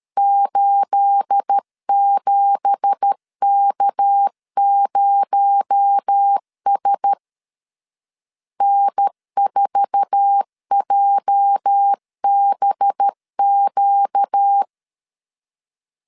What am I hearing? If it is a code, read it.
Morse code, heard as a keyed tone, about 13 wpm, 800 Hz. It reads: 87K0S N4JBQ
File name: LEZIONI / 9 2867 12wpm 2322